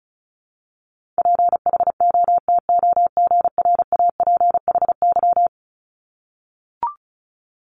Morse code, heard as a keyed tone, about 36 wpm, 700 Hz. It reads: PHOTOGRAPHY
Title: Multi-letter Phonemes 35wpm